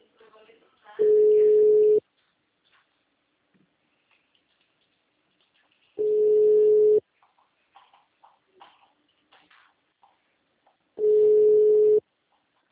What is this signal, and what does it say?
Ringback tone (Europe, 425 Hz), with speech.
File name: out- -2013- - - .36009